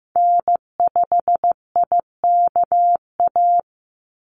Morse code, heard as a keyed tone, about 15 wpm, 700 Hz. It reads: N5IKA
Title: USA Call Signs 15wpm